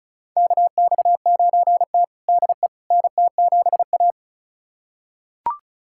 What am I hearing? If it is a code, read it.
Morse code, heard as a keyed tone, about 36 wpm, 700 Hz. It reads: KX9T DE NT7A
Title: USA Callsign DE Callsign - 4 Characters - 35wpm